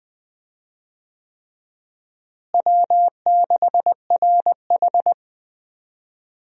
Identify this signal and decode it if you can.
Morse code, heard as a keyed tone, about 20 wpm, 700 Hz. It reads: W6RH
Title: USA Call Signs - 4 Characters - Repeated 3x - 20wpm